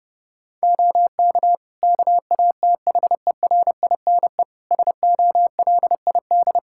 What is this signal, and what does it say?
Morse code, heard as a keyed tone, about 32 wpm, 700 Hz. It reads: OK KATHERINE SOLID
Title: Practice QSO - Exchange 3 - More Info - 30wpm